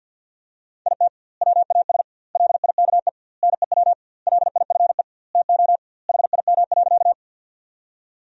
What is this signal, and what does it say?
Morse code, heard as a keyed tone, about 52 wpm, 700 Hz. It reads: IT WAS LIKE NEW LIFE TO HIM.
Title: 2x Word-Spacing - Sentences from Top 300 Words - 50wpm